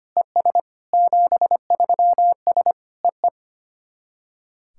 Morse code, heard as a keyed tone, about 25 wpm, 700 Hz. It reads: ES 73S EE
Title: SOTA QSO Practice 25wpm